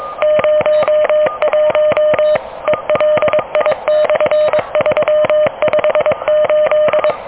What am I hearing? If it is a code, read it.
Morse code, heard as a keyed tone, about 24 wpm, 610 Hz. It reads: 01 ELI/358
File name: OK QRP 2023 / OK QRP 2023